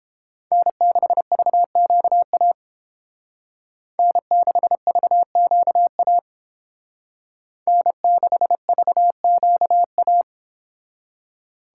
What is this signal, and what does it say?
Morse code, heard as a keyed tone, about 31 wpm, 700 Hz. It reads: N6VQA N6VQA N6VQA